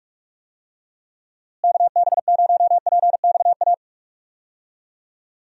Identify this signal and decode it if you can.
Morse code, heard as a keyed tone, about 46 wpm, 700 Hz. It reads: KB0PXA
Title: USA Call Signs - 6 Characters - Repeated 3x - 45wpm